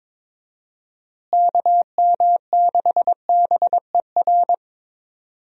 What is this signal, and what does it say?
Morse code, heard as a keyed tone, about 22 wpm, 700 Hz. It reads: KM6BER
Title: USA Call Signs - 6 Characters - Repeated 3x - 22wpm